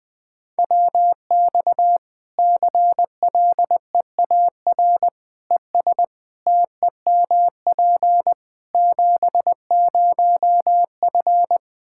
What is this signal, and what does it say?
Morse code, heard as a keyed tone, about 20 wpm, 700 Hz. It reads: WX CLEAR ES TEMP 70F
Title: Practice QSO - Exchange 2 - Info - 20wpm